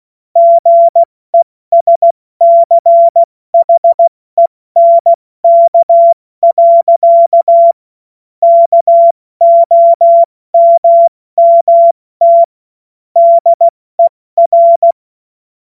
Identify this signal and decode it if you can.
Morse code, heard as a keyed tone, about 16 wpm, 680 Hz. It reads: GESCHENK. KOMMT DER
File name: Morsekurs / Witze / Witz 09. November 2021, 70 F80